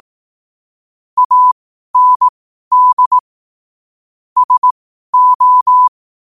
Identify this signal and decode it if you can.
Morse code, heard as a keyed tone, about 18 wpm, spaced as for 12 wpm, 1000 Hz. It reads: AND SO
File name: qotdfiles / 12wpm